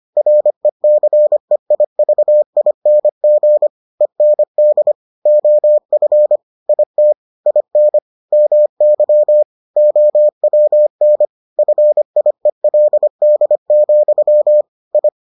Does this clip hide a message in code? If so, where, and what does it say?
Yes: Morse code from 0.2 to 15.1 s, heard as a keyed tone, about 25 wpm, 600 Hz. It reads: RECEIVING END OF IT IN MY OWN FIELD, I